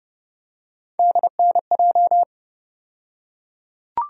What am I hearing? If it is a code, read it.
Morse code, heard as a keyed tone, about 30 wpm, 700 Hz. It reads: DNJ
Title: USA Call Sign Postfixes - 3 Letters 30wpm